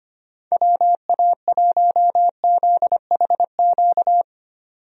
Morse code, heard as a keyed tone, about 25 wpm, 700 Hz. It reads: WA1ZHQ
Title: USA Call Signs - 6 Characters - 25wpm